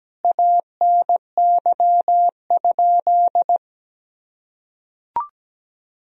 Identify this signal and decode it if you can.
Morse code, heard as a keyed tone, about 17 wpm, 700 Hz. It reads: ANY?
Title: Sentences from Top 300 Words and Abbreviations 17wpm